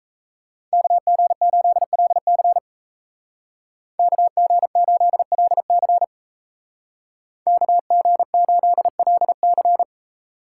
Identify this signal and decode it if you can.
Morse code, heard as a keyed tone, about 36 wpm, 700 Hz. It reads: KG8LC KG8LC KG8LC